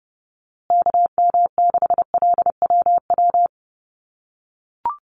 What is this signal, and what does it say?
Morse code, heard as a keyed tone, about 30 wpm, 700 Hz. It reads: KM6LWW
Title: USA Call Signs Repeated 2x 30wpm